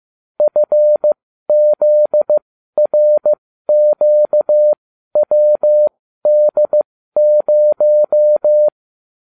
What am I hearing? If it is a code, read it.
Morse code, heard as a keyed tone, about 15 wpm, spaced as for 12 wpm, 600 Hz. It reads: FZRQWD0